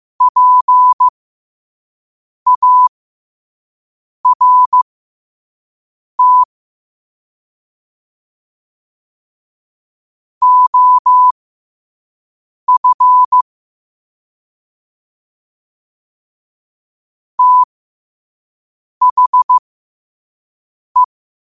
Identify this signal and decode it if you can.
Morse code, heard as a keyed tone, about 15 wpm, spaced as for 5 wpm, 1000 Hz. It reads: PART OF THE
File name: qotdfiles / 5wpm